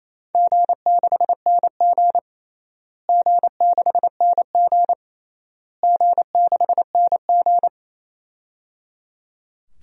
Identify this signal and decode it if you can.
Morse code, heard as a keyed tone, about 28 wpm, 700 Hz. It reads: G6NG G6NG G6NG
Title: DXCC-100 Call Signs - 4 Characters - Repeated 3x - 28wpm